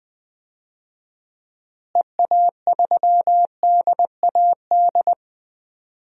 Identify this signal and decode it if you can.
Morse code, heard as a keyed tone, about 20 wpm, 700 Hz. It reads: EA3DAD
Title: DXCC-100 Call Signs - 6 Characters - Repeated 3x - 20wpm